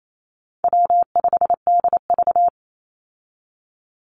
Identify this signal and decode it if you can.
Morse code, heard as a keyed tone, about 28 wpm, 700 Hz. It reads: W5DV
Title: Easy Call Signs Repeated 3x - 28wpm